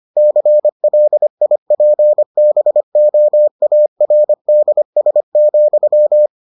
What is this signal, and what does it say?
Morse code, heard as a keyed tone, about 25 wpm, 600 Hz. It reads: CLIPBOARDS,